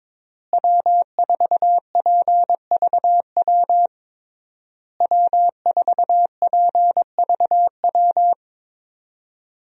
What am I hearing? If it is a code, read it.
Morse code, heard as a keyed tone, about 22 wpm, 700 Hz. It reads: W4PVW W4PVW
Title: USA Call Signs - 5 Characters - Repeated 3x - 22wpm